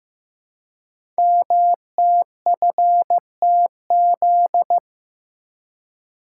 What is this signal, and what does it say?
Morse code, heard as a keyed tone, about 15 wpm, 700 Hz. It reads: MTFTZ
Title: Code Groups - 5 Letter:Numbers - 15wpm